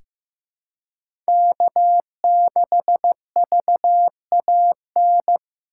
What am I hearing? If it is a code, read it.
Morse code, heard as a keyed tone, about 15 wpm, 700 Hz. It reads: K6VAN
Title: USA Call Signs - 5 Characters - 4x Character Spacing - 15wpm